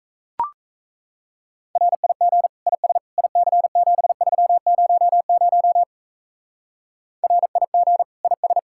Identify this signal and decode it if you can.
Morse code, heard as a keyed tone, about 32 wpm, 700 Hz. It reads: RIG IS IC7300 RIG IS
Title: Speed-Racing Common QSO Phrases Encore - 28wpm